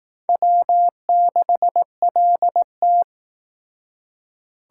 Morse code, heard as a keyed tone, about 19 wpm, 700 Hz. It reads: W6LT
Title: Speed Racing - USA 1x2 Call Signs 15wpm